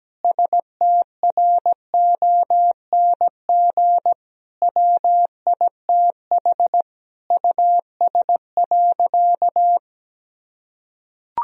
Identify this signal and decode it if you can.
Morse code, heard as a keyed tone, about 17 wpm, 700 Hz. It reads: STRONG WITH US.
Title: Sentences from Top 1000 Words - Review Part 3 - 17wpm